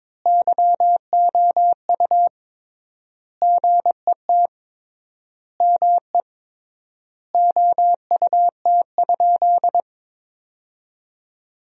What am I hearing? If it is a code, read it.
Morse code, heard as a keyed tone, about 22 wpm, 700 Hz. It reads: YOU GET ME OUT?
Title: 3x Word-Spacing - Sentences from Top 200 Words - 22wpm